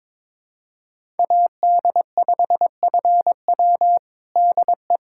Morse code, heard as a keyed tone, about 21 wpm, 700 Hz. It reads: AD5FW DE
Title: USA Callsign DE Callsign - 5 Characters - 22wpm